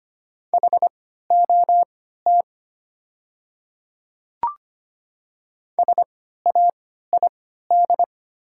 Morse code, heard as a keyed tone, about 25 wpm, spaced as for 14 wpm, 700 Hz. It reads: HOT SAID
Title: 3x Character Spacing - Top 100 Words - 25wpm